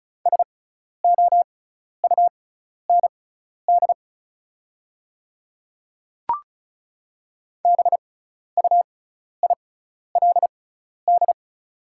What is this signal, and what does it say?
Morse code, heard as a keyed tone, about 35 wpm, spaced as for 12 wpm, 690 Hz. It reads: SOUND BUILD
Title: 6x Character Spacing - Top 300 Words - 35wpm